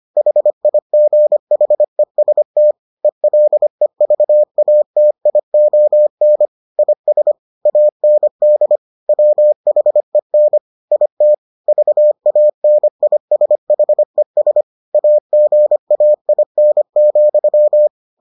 Morse code, heard as a keyed tone, about 25 wpm, 600 Hz. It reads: HIGHEST ELEVATION IS AND WHEN IT VANISHES AGAIN,